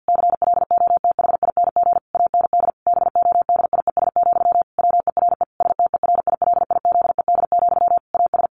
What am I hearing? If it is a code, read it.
Morse code, heard as a keyed tone, about 50 wpm, 700 Hz. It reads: CLOTHING AND BODIES, WERE STERILIZED, AS
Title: A Sound Of Thunder 50wpm